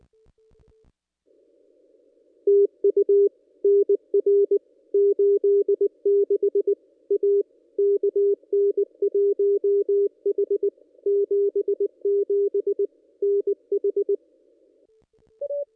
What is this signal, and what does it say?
Morse code, heard as a keyed tone, about 20 wpm, 410 Hz. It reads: TU NR 86 A KN1H 77 NH
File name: contests / ss-cw-k5tr-op-n5rz -1900